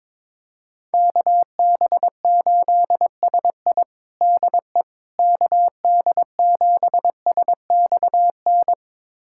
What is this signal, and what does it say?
Morse code, heard as a keyed tone, about 22 wpm, 700 Hz. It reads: KB8SI DE KD7SXN
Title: USA Callsign DE Callsign 22wpm